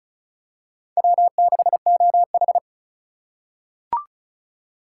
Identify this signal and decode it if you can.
Morse code, heard as a keyed tone, about 36 wpm, 700 Hz. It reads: W6OH
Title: USA Call Signs - 4 Characters - 2x Character Spacing - 35wpm